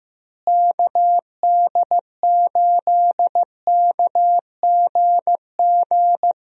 Morse code, heard as a keyed tone, about 15 wpm, 700 Hz. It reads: KD8KGG
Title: USA Call Signs - 6 Characters - Repeated 3x - 15wpm